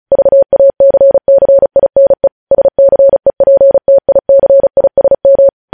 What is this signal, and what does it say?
Morse code, heard as a keyed tone, about 36 wpm, 550 Hz. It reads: VACCINE SCEPTICISM
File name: podcast / News headlines-1x-35x35